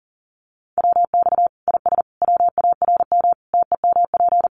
Morse code, heard as a keyed tone, about 40 wpm, 700 Hz. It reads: WX IS WARM TEMP
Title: Common QSO Phrases 40wpm